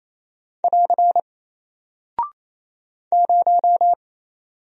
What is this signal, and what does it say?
Morse code, heard as a keyed tone, about 27 wpm, 700 Hz. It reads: <AR> 0
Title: Single Character (Letter:Number:Punctuation:Prosign) 28wpm